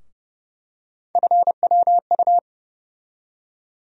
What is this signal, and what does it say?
Morse code, heard as a keyed tone, about 30 wpm, 700 Hz. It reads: FWU
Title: USA Call Sign Postfixes - 3 Letters 30wpm